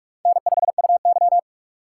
Morse code, heard as a keyed tone, about 46 wpm, 700 Hz. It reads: N5UY
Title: USA Call Signs - 4 Characters - 4x Character Spacing - 45wpm